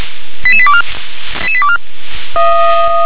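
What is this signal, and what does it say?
Five-tone selective call (ZVEI), heard as quick runs of five tones, tones of about 70 ms. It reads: 88824 88824
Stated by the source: ZVEI five-tone selective call received with an SDR.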